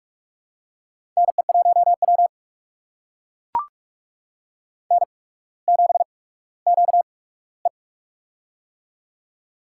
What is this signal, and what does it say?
Morse code, heard as a keyed tone, about 42 wpm, spaced as for 19 wpm, 700 Hz. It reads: NE1W N7QE
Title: USA Call Signs - 4 Characters - 8x Character Spacing - 45wpm